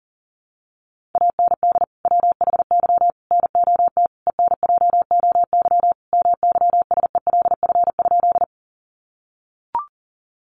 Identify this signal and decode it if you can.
Morse code, heard as a keyed tone, about 40 wpm, 700 Hz. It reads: AND WHY NOT ENJOY MYSELF?
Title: Sentences from Top 1300 Words 40wpm